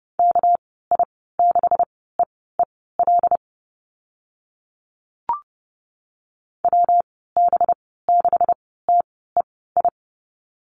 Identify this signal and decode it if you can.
Morse code, heard as a keyed tone, about 30 wpm, spaced as for 17 wpm, 700 Hz. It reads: KI6EEL WB6TEI
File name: Lesson 37g - Letter x - Call Signs 30wpm